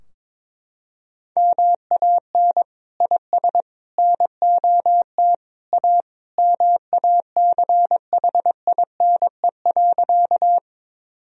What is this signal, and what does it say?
Morse code, heard as a keyed tone, about 22 wpm, 700 Hz. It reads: MAN IS NOT A MACHINE.
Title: Sentences from Top 1000 Words - Review Part 2 - 22wpm